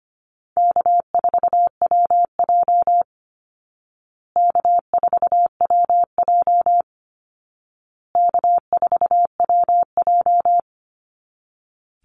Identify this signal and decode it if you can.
Morse code, heard as a keyed tone, about 25 wpm, 700 Hz. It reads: K4WJ K4WJ K4WJ
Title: Easy Call Signs Repeated 3x - 25wpm